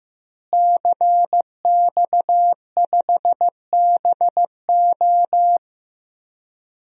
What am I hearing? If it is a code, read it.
Morse code, heard as a keyed tone, about 15 wpm, 700 Hz. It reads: CX5BO
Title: DXCC-100 Call Signs - 5 Characters - Repeated 3x - 15wpm